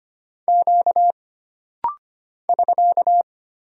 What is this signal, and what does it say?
Morse code, heard as a keyed tone, about 25 wpm, 700 Hz. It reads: Q <SK>
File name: Single Character (Letter:Number:Punctuation:Prosign) - Rapid-Fire - 25wpm